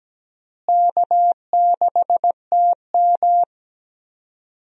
Morse code, heard as a keyed tone, about 17 wpm, 700 Hz. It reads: K6TM
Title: USA Call Signs - 4 Characters - Repeated 2x - 17wpm